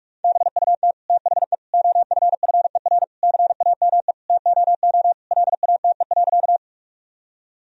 Morse code, heard as a keyed tone, about 46 wpm, 700 Hz. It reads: BUT THE OFFER CAME TOO LATE.
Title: Sentences from Top 1000 Words - Review Part 4 - 45wpm